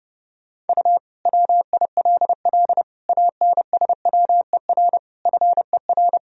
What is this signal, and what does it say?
Morse code, heard as a keyed tone, about 30 wpm, 700 Hz. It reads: U WILL ANSWER FER
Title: Sentences from Top 300 Words and Abbreviations 30wpm